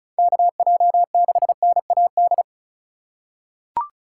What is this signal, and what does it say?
Morse code, heard as a keyed tone, about 36 wpm, 700 Hz. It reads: KJ6NAD
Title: USA Call Signs - 6 Characters - 3x Character Spacing - 35wpm